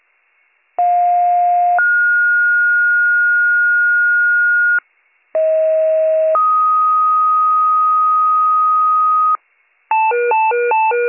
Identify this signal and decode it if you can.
Two-tone page, heard as a one-second tone, then a three-second tone, about 710 then 1430 Hz, about 640 then 1250 Hz, with noise. It is heard from 0.8 to 9.4 s.